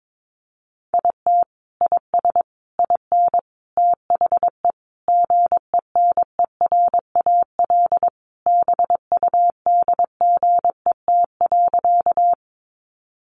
Morse code, heard as a keyed tone, about 22 wpm, 700 Hz. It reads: IT IS IN THE GENERAL BUDGET.
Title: Sentences from Top 1200 Words 22wpm